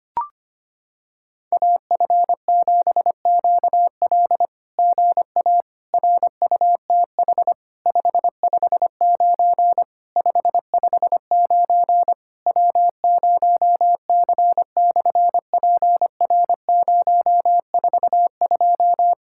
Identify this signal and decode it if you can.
Morse code, heard as a keyed tone, about 25 wpm, 700 Hz. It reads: AF7QL GA RUTH 559 559 W0C/PR042
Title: SOTA QSO Practice 25wpm